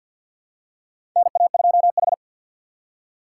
Morse code, heard as a keyed tone, about 49 wpm, 700 Hz. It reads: NA2H
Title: USA Call Signs - 4 Characters - 2x Character Spacing - 50wpm